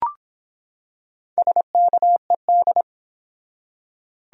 Morse code, heard as a keyed tone, about 24 wpm, 700 Hz. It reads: SKED